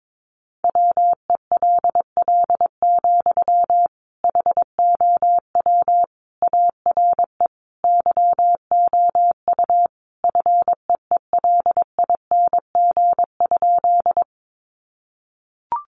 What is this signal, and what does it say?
Morse code, heard as a keyed tone, about 22 wpm, 700 Hz. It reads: WELL, HOW ARE YOU FEELING?
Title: Sentences from Top 1200 Words 22wpm